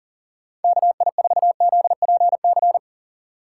Morse code, heard as a keyed tone, about 37 wpm, 700 Hz. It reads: KI4ZPC
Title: USA Call Signs - 6 Characters - 4x Character Spacing - 40wpm